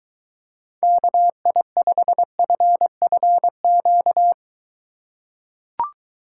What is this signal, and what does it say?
Morse code, heard as a keyed tone, about 22 wpm, 700 Hz. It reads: KI5FFQ